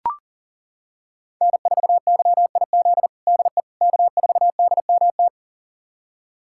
Morse code, heard as a keyed tone, about 40 wpm, 700 Hz. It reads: N4YIZ DE K4DMT
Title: USA Callsign DE Callsign - 5 Characters - 40wpm